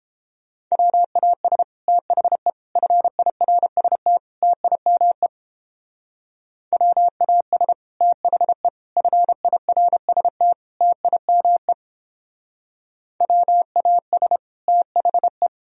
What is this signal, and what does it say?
Morse code, heard as a keyed tone, about 31 wpm, 700 Hz. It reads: WAS THE FIRST TIME WAS THE FIRST TIME WAS THE